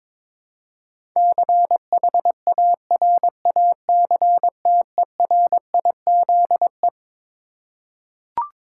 Morse code, heard as a keyed tone, about 22 wpm, 700 Hz. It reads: CHARACTERIZE
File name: Twelve Letter Words - 22wpm